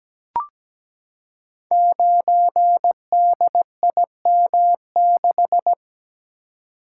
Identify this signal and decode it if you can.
Morse code, heard as a keyed tone, about 17 wpm, 700 Hz. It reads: 9DIM6
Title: Code Groups - 5 Letter:Numbers - 17wpm